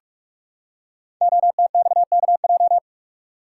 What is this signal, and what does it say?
Morse code, heard as a keyed tone, about 44 wpm, 700 Hz. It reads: OTXKJ
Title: Code Groups - 5 Letter:Numbers - 45wpm